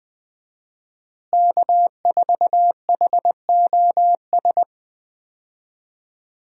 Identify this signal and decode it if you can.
Morse code, heard as a keyed tone, about 20 wpm, 700 Hz. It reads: K4HOS
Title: USA Call Signs - 5 Characters - Repeated 3x - 20wpm